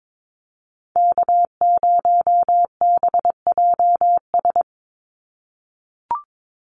Morse code, heard as a keyed tone, about 22 wpm, 700 Hz. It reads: K0BJS
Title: USA Call Signs 22wpm